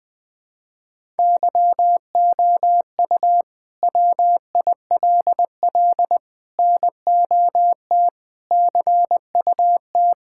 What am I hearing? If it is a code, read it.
Morse code, heard as a keyed tone, about 20 wpm, 700 Hz. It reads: YOU WILL NOT CUT
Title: Sentences from Top 1000 Words - Review Part 6 - 20wpm